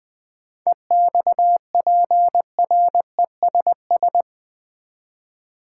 Morse code, heard as a keyed tone, about 21 wpm, 700 Hz. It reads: EXPRESS
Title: Sentences from Top 1100 Words - New Word Drill Alt v2 - 20wpm